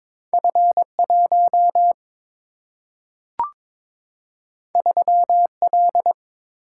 Morse code, heard as a keyed tone, about 22 wpm, 700 Hz. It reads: F1 3L
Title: Single-Letter Single-Number 22wpm